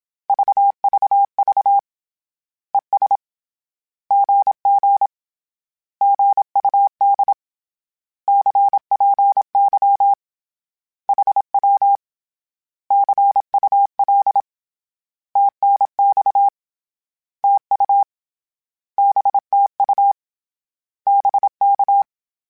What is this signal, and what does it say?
Morse code, heard as a keyed tone, about 27 wpm, 800 Hz. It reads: VVV ES GG GUD CPY HW CUL TNX TU BTU BK
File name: Short Runs / Short Runs-mp3s / Words @25